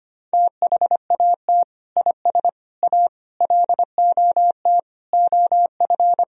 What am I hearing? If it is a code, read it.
Morse code, heard as a keyed tone, about 25 wpm, 700 Hz. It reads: THAT IS A LOT OF
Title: Sentences from Top 1000 Words - Review Part 3 - 25wpm